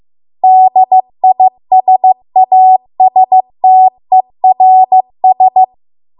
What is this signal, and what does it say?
Morse code, heard as a keyed tone, about 15 wpm, 750 Hz. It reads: DISASTERS